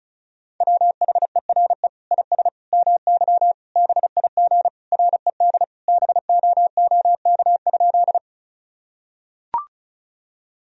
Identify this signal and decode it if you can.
Morse code, heard as a keyed tone, about 36 wpm, 700 Hz. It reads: WHERE IS MY BIG RED BOOK?
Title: Sentences from Top 1000 Words - Review Part 2 - 35wpm